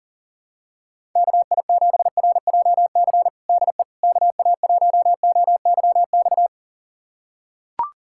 Morse code, heard as a keyed tone, about 40 wpm, 700 Hz. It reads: KI7RJC DE KA1OYX
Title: USA Callsign DE Callsign - 6 Characters - 40wpm